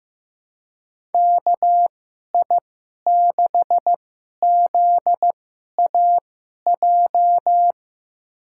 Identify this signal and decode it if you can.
Morse code, heard as a keyed tone, about 15 wpm, spaced as for 11 wpm, 700 Hz. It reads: KI6ZAJ
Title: USA Call Signs - 6 Characters - 2x Character Spacing - 15wpm